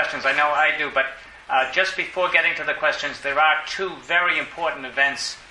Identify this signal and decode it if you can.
Emergency Alert System (SAME) tones, heard as harsh buzzing bursts, with speech. It is heard from 0.9 to 4.5 s.